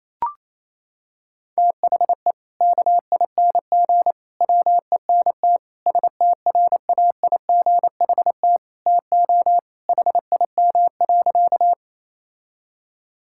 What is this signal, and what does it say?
Morse code, heard as a keyed tone, about 28 wpm, 700 Hz. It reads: THE KING WENT STRAIGHT TO HIM.
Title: Sentences from Top 1000 Words - Review Part 6 - 28wpm